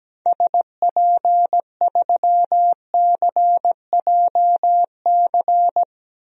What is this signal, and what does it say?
Morse code, heard as a keyed tone, about 17 wpm, 700 Hz. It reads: SP3CJC
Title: DXCC-100 Call Signs - 6 Characters - Repeated 3x - 17wpm